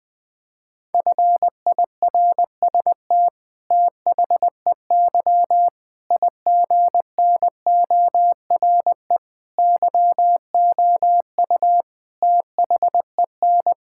Morse code, heard as a keyed tone, about 20 wpm, 700 Hz. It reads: FIRST THEY IGNORE YOU THEN
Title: Sentences with 8-Word Phrases - 20wpm